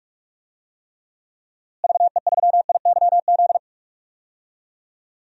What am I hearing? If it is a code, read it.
Morse code, heard as a keyed tone, about 42 wpm, 700 Hz. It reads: VE3IYZ
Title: DXCC-100 Call Signs - 6 Characters - Repeated 3x - 45wpm